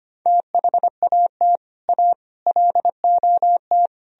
Morse code, heard as a keyed tone, about 33 wpm, 700 Hz. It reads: THAT A LOT